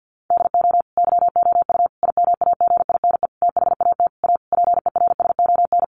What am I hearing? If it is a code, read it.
Morse code, heard as a keyed tone, about 50 wpm, 700 Hz. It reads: DO YOU IMAGINE THAT A PERSON
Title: Sentences from Top 1400 Words 50wpm